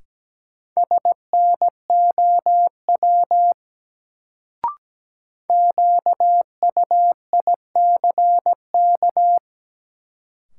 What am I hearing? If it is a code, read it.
Morse code, heard as a keyed tone, about 17 wpm, 700 Hz. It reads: SNOW QUICK
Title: Sentences from Top 500 Words - New Word Drill - 17wpm